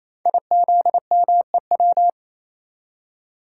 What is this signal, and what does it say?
Morse code, heard as a keyed tone, about 29 wpm, 700 Hz. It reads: IZMEW
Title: Code Groups - 5 Letters - 28wpm